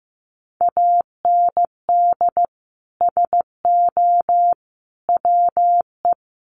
Morse code, heard as a keyed tone, about 15 wpm, 700 Hz. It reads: AND SO WE
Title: Sentences from Top 1500 Words 15wpm